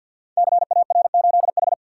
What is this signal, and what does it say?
Morse code, heard as a keyed tone, about 53 wpm, 700 Hz. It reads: CAR8H
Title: Code Groups - 5 Letter:Numbers - 50wpm